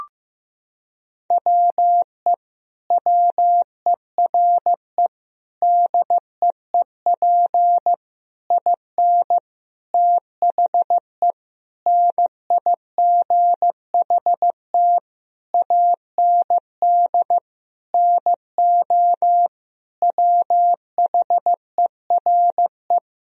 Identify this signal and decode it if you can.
Morse code, heard as a keyed tone, about 15 wpm, 700 Hz. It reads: WE WERE DEEP IN THE NIGHT AND NO WHERE
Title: Sentences from Top 1000 Words - Review Part 3 - 15wpm